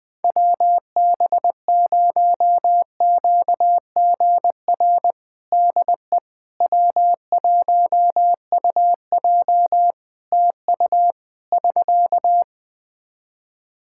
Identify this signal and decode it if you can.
Morse code, heard as a keyed tone, about 20 wpm, 700 Hz. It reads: WB0QGR DE W1UJ TU <SK>
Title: Practice QSO - Exchange 4 - Ending - 20wpm